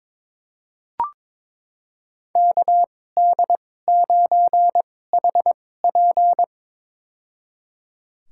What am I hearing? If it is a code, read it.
Morse code, heard as a keyed tone, about 22 wpm, spaced as for 16 wpm, 700 Hz. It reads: KD9HP
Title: USA Call Signs - 5 Characters - 2x Character Spacing - 22wpm